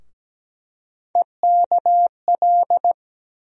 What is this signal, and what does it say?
Morse code, heard as a keyed tone, about 17 wpm, 700 Hz. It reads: EKL
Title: USA Call Sign Postfixes - 3 Letters 17wpm